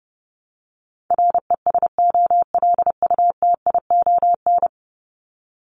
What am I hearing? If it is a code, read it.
Morse code, heard as a keyed tone, about 29 wpm, 700 Hz. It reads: RESOLUTION
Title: Top 1501-1750 Words 30wpm